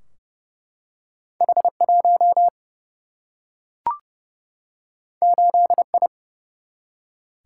Morse code, heard as a keyed tone, about 30 wpm, 700 Hz. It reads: H 1 8 I
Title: Single-Letter Single-Number 30wpm